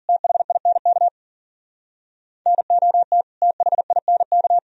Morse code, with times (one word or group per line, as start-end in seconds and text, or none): 0.1-1.1 THINK
2.5-3.2 NOT
3.4-4.6 THINK